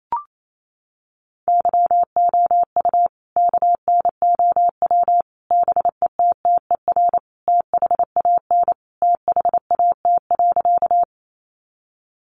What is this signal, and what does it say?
Morse code, heard as a keyed tone, about 28 wpm, 700 Hz. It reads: YOU KNOW BETTER THAN THAT.
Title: Sentences from Top 400 Words 28wpm